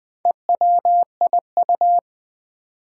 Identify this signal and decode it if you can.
Morse code, heard as a keyed tone, about 20 wpm, 700 Hz. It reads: EWIU
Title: Four Challenging Letter-Number Combinations 20wpm